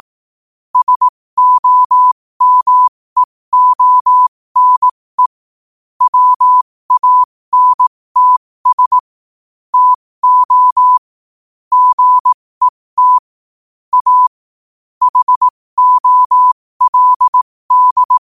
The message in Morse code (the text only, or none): SOMEONE WANTS TO GET A HOLD